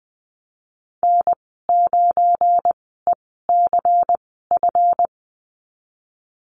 Morse code, heard as a keyed tone, about 20 wpm, spaced as for 15 wpm, 700 Hz. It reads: N9ECF
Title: 186 Lesson 30g - Letter b - Call Signs 20wpm